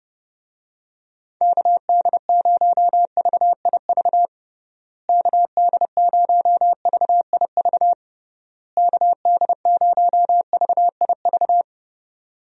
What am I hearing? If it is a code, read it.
Morse code, heard as a keyed tone, about 30 wpm, 700 Hz. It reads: KD0VIV KD0VIV KD0VIV
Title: USA Call Signs - 6 Characters - Repeated 3x - 30wpm